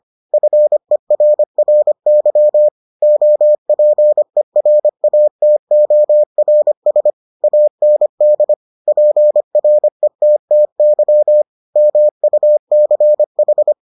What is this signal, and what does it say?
Morse code, heard as a keyed tone, about 25 wpm, 600 Hz. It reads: FERRY OPERATORS AND PRETTY MUCH